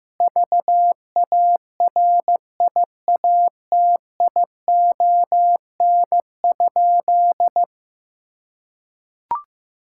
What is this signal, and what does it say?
Morse code, heard as a keyed tone, about 15 wpm, 700 Hz. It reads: VARIATION?
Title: Sentences from Top 1600 Words 15wpm